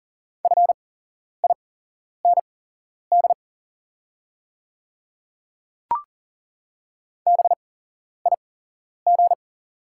Morse code, heard as a keyed tone, about 40 wpm, spaced as for 11 wpm, 690 Hz. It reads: FIND BIG
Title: 8x Character Spacing - Top 500 Words - 40wpm